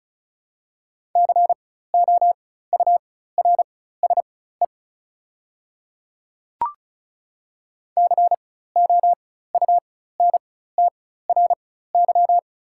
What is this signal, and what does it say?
Morse code, heard as a keyed tone, about 34 wpm, spaced as for 16 wpm, 690 Hz. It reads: COURSE COUNTRY
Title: 4x Character Spacing - Top 500 Words - 35wpm